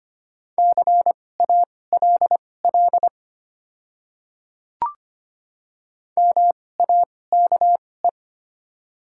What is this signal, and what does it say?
Morse code, heard as a keyed tone, about 25 wpm, spaced as for 18 wpm, 690 Hz. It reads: CALL MAKE
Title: 2x Character Spacing - Top 200 Words - 25wpm